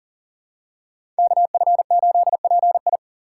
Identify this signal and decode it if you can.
Morse code, heard as a keyed tone, about 40 wpm, 700 Hz. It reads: KF8PI
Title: USA Call Signs - 5 Characters - 4x Character Spacing - 40wpm